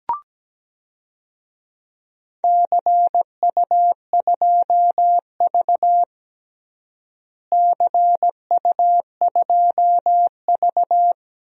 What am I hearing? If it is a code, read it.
Morse code, heard as a keyed tone, about 17 wpm, 700 Hz. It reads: CU2V CU2V
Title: DXCC-100 Call Signs - 4 Characters - Repeated 2x - 17wpm